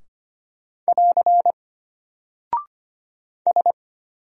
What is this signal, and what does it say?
Morse code, heard as a keyed tone, about 25 wpm, 700 Hz. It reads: <AR> S
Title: Single Character (Letter:Number:Punctuation:Prosign) Alt 25wpm